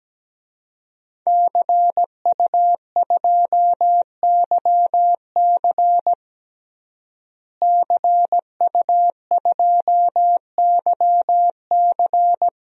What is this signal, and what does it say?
Morse code, heard as a keyed tone, about 17 wpm, 700 Hz. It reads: CU2YC CU2YC
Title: DXCC-100 Call Signs - 5 Characters - Repeated 2x - 17wpm